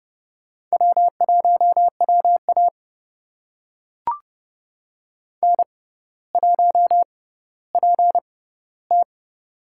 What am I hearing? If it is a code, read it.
Morse code, heard as a keyed tone, about 30 wpm, spaced as for 15 wpm, 700 Hz. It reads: W1WA N1PT
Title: USA Call Signs - 4 Characters - 6x Character Spacing - 30wpm